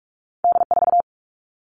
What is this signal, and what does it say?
Morse code, heard as a keyed tone, about 46 wpm, 700 Hz. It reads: D4
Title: All Cut Numbers - 2-Digit 45wpm